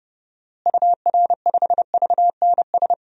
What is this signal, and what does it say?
Morse code, heard as a keyed tone, about 30 wpm, 700 Hz. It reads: UR5VNS
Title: DXCC-100 Call Signs - 6 Characters - Repeated 2x - 30wpm